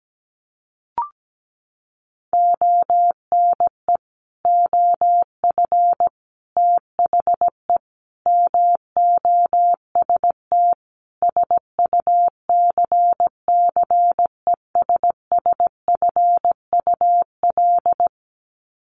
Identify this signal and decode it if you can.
Morse code, heard as a keyed tone, about 17 wpm, 700 Hz. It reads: ONE OF THE MOST SUCCESSFUL
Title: Sets of 5 Words Encore2 17wpm